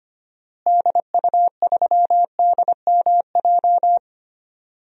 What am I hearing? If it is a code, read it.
Morse code, heard as a keyed tone, about 25 wpm, 700 Hz. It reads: DU3DMJ
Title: DXCC-100 Call Signs - 6 Characters - Repeated 2x - 25wpm